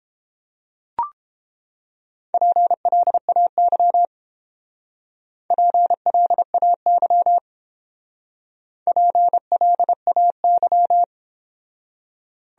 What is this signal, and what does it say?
Morse code, heard as a keyed tone, about 28 wpm, 700 Hz. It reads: PLAY PLAY PLAY